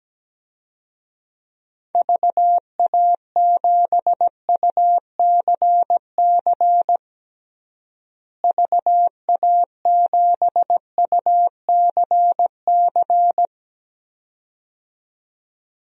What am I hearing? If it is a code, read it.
Morse code, heard as a keyed tone, about 17 wpm, 700 Hz. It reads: VA7UCC VA7UCC
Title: DXCC-100 Call Signs - 6 Characters - Repeated 2x - 17wpm